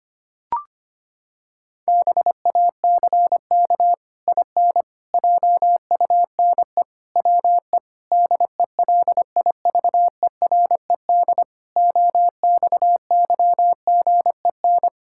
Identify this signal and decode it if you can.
Morse code, heard as a keyed tone, about 25 wpm, 700 Hz. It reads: BACK IN JUNE WE DELIVERED OXYGEN
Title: Pangram Challenge v2 - 25wpm